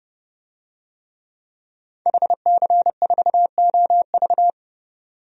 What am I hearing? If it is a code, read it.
Morse code, heard as a keyed tone, about 30 wpm, 700 Hz. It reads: HC4OV
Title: DXCC-100 Call Signs - 5 Characters - 30wpm